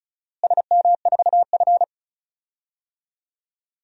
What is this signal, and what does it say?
Morse code, heard as a keyed tone, about 34 wpm, 700 Hz. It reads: SM4F
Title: DXCC-100 Call Signs - 4 Characters - Repeated 3x - 35wpm